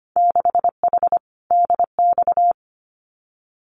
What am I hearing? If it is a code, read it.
Morse code, heard as a keyed tone, about 25 wpm, 700 Hz. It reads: 6H DX
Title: Winter Field Day Exchanges Encore 25wpm